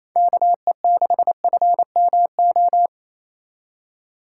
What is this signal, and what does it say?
Morse code, heard as a keyed tone, about 35 wpm, 700 Hz. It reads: KE6FMO
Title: Speed Racing - USA Call Signs 28wpm